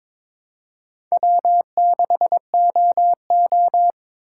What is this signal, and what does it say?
Morse code, heard as a keyed tone, about 21 wpm, 700 Hz. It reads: W6OO
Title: USA Call Signs - 4 Characters - 2x Character Spacing - 22wpm